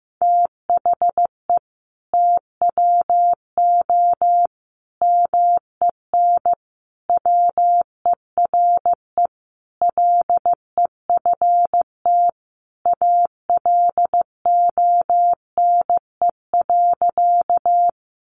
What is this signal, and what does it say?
Morse code, heard as a keyed tone, about 15 wpm, 700 Hz. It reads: THE TWO MEN WERE LEFT ALONE.
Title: Sentences from Top 1000 Words 15wpm